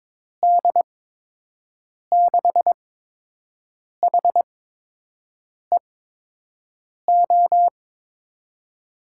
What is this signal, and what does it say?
Morse code, heard as a keyed tone, about 22 wpm, spaced as for 6 wpm, 700 Hz. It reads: D6HEO
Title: USA Call Signs - 8x Character Spacing - 22wpm